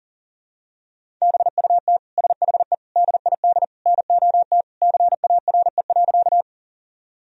Morse code, heard as a keyed tone, about 40 wpm, 700 Hz. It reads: BUT SHE DID NOT CARE.
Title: Sentences from Top 1000 Words - Review Part 1 - 40wpm